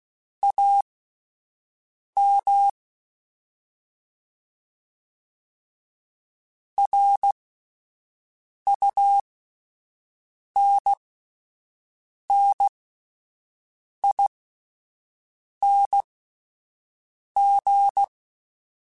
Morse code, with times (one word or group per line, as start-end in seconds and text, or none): 0.4-2.7 AM
6.8-18.0 RUNNING